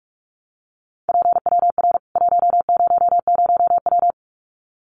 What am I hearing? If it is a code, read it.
Morse code, heard as a keyed tone, about 44 wpm, 700 Hz. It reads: PWR 100W
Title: Common QSO Phrases 45wpm